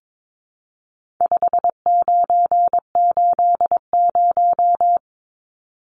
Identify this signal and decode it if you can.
Morse code, heard as a keyed tone, about 22 wpm, 700 Hz. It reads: 5980
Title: Numbers - 4-Digit 22wpm